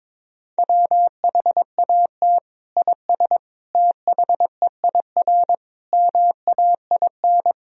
Morse code, heard as a keyed tone, about 21 wpm, 700 Hz. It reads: WHAT IS THEIR MAIN
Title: Sentences from Top 1000 Words - Review Part 2 - 22wpm